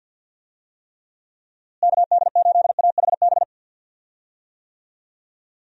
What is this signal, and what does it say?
Morse code, heard as a keyed tone, about 52 wpm, 700 Hz. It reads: KD8AHB
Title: USA Call Signs - 6 Characters - Repeated 2x - 50wpm